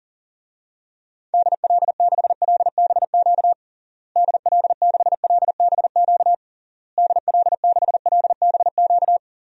Morse code, heard as a keyed tone, about 40 wpm, 700 Hz. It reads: DL6LBQ DL6LBQ DL6LBQ
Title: DXCC-100 Call Signs - 6 Characters - Repeated 3x - 40wpm